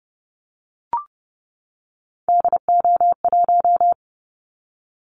Morse code, heard as a keyed tone, about 30 wpm, 700 Hz. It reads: DO1
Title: Callsign Prefixes for Top 25 DXCC - 30wpm